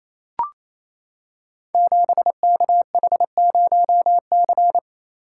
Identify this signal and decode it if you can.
Morse code, heard as a keyed tone, about 28 wpm, 700 Hz. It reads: 7KH0C
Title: Code Groups - 5 Letter:Numbers - 28wpm